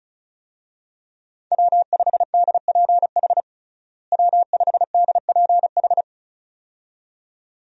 Morse code, heard as a keyed tone, about 36 wpm, 700 Hz. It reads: W5DPH W5DPH
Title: USA Call Signs - 5 Characters - Repeated 2x - 35wpm